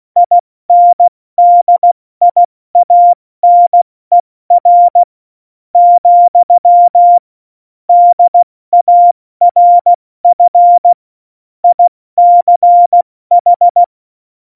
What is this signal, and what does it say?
Morse code, heard as a keyed tone, about 16 wpm, spaced as for 13 wpm, 700 Hz. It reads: INDIANER, DARF ICH
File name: Morsekurs / Witze / Witz 06. Oktober 2020, 70 F80